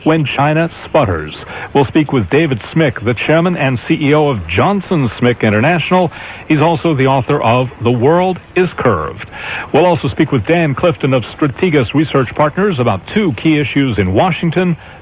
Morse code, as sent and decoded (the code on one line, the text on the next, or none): none